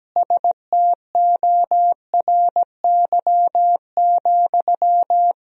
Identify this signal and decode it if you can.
Morse code, heard as a keyed tone, about 17 wpm, 700 Hz. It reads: STORY,
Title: Sentences from Top 1000 Words - Review Part 2 - 17wpm